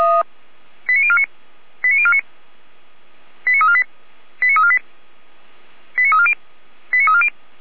Five-tone selective call (ZVEI), heard as quick runs of five tones, tones of about 70 ms. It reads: 89049 89049 89348 89348 89340 89340